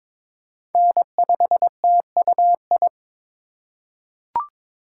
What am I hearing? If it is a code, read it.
Morse code, heard as a keyed tone, about 22 wpm, 700 Hz. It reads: N5TUI
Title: USA Call Signs - 5 Characters - 3x Character Spacing - 22wpm